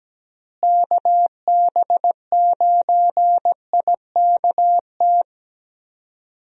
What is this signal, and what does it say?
Morse code, heard as a keyed tone, about 17 wpm, 700 Hz. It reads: KB9IKT
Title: USA Call Signs - 6 Characters - Repeated 3x - 17wpm